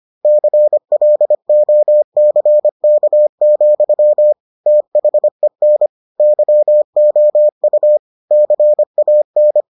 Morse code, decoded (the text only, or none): CLOCK, THEN YOU CAN